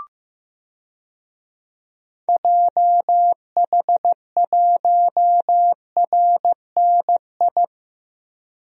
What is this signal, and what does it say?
Morse code, heard as a keyed tone, about 15 wpm, 700 Hz. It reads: JH1RNI
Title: DXCC-100 Call Signs - 6 Characters - 15wpm